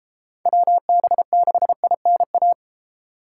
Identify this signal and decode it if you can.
Morse code, heard as a keyed tone, about 36 wpm, 700 Hz. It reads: WB6INA